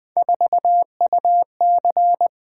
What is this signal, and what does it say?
Morse code, heard as a keyed tone, about 20 wpm, 700 Hz. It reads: 4UC
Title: Three Letter-Number Combinations - 20wpm